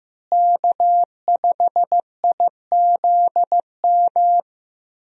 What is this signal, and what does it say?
Morse code, heard as a keyed tone, about 15 wpm, 700 Hz. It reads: K5IZM
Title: USA Call Signs - 5 Characters - Repeated 3x - 15wpm